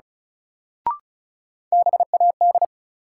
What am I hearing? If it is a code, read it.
Morse code, heard as a keyed tone, about 36 wpm, 700 Hz. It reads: BAD
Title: Sentences from Top 700 Words - New Word Drill - 35wpm